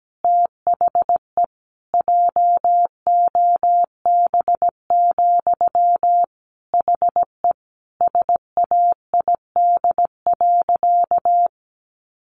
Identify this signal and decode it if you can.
Morse code, heard as a keyed tone, about 17 wpm, 700 Hz. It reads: THE JOB, HE SAID.
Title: Sentences from Top 1500 Words 17wpm